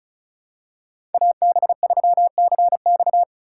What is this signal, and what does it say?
Morse code, heard as a keyed tone, about 36 wpm, 700 Hz. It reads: AB3CX
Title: USA Call Signs - 5 Characters - 2x Character Spacing - 35wpm